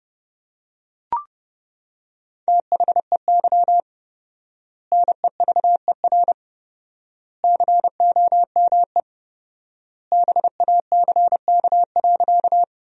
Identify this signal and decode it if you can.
Morse code, heard as a keyed tone, about 30 wpm, 700 Hz. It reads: THEY NEVER COME BACK.
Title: 4x Word-Spacing - Sentences from Top 300 Words - 30wpm